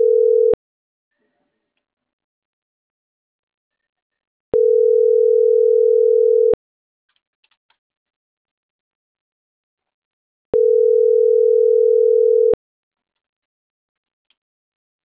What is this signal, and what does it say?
Ringback tone (North America), 440 + 480 Hz.